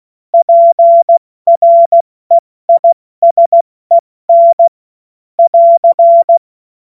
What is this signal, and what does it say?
Morse code, heard as a keyed tone, about 16 wpm, 680 Hz. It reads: PREISEN <AR>
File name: Morsekurs / Witze / Witz 21. Februar 2023 70 F80